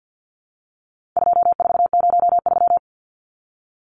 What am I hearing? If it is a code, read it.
Morse code, heard as a keyed tone, about 50 wpm, 700 Hz. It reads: 2403
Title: Numbers - 4-Digit 50wpm